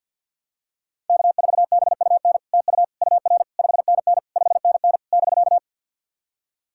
Morse code, heard as a keyed tone, about 50 wpm, 700 Hz. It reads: K4BUN TU UR 5NN 5NN <BK>
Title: SOTA QSO Practice 50wpm